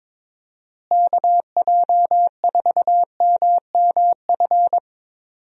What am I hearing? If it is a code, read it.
Morse code, heard as a keyed tone, about 22 wpm, 700 Hz. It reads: KJ4MMF
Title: USA Call Signs - 6 Characters - Repeated 3x - 22wpm